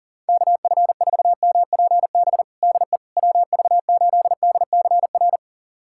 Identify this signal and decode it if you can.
Morse code, heard as a keyed tone, about 40 wpm, 700 Hz. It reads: KF4MPB DE WV8DCR
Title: USA Callsign DE Callsign 40wpm